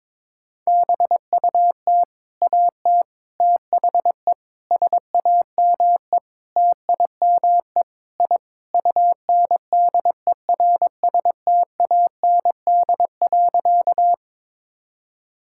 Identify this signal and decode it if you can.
Morse code, heard as a keyed tone, about 22 wpm, 700 Hz. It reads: BUT AT THE SAME TIME I UNDERSTAND.
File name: practice / 22wpm / Sentences with 4-Word Phrases - 22wpm